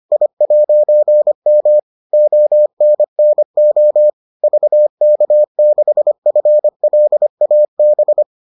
Morse code, heard as a keyed tone, about 25 wpm, 600 Hz. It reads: I'M ONNO VK6FLAB